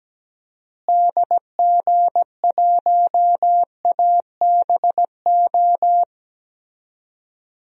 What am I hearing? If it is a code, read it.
Morse code, heard as a keyed tone, about 17 wpm, 700 Hz. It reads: DG1ABO
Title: DXCC-100 Call Signs - 6 Characters - Repeated 3x - 17wpm